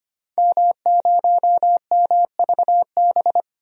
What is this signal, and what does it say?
Morse code, heard as a keyed tone, about 25 wpm, 700 Hz. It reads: M0MVB
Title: DXCC-100 Call Signs - 5 Characters - 25wpm